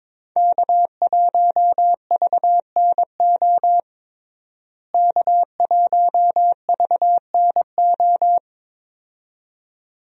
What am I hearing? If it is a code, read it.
Morse code, heard as a keyed tone, about 22 wpm, 700 Hz. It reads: K1VNO K1VNO
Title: USA Call Signs - 5 Characters - Repeated 2x - 22wpm